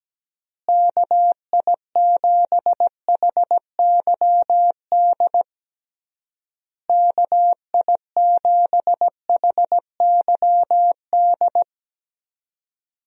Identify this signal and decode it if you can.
Morse code, heard as a keyed tone, about 17 wpm, 700 Hz. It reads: KI7HYD KI7HYD
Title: USA Call Signs - 6 Characters - Repeated 3x - 17wpm